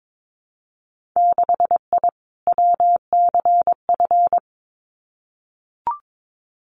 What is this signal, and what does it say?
Morse code, heard as a keyed tone, about 22 wpm, 700 Hz. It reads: 6I WCF
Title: Winter Field Day Exchanges Encore 22wpm